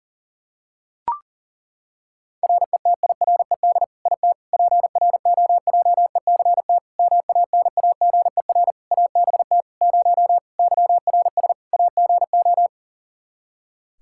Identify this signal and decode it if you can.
Morse code, heard as a keyed tone, about 37 wpm, 700 Hz. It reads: RETIRED IT PROJECT MANAGER ABT 0 YRS AGO
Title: Practice QSO - Exchange 3 - More Info - 40wpm